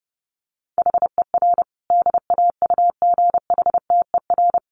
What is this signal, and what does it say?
Morse code, heard as a keyed tone, about 28 wpm, 700 Hz. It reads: HER DAUGHTER
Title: Sentences from Top 1500 Words 30wpm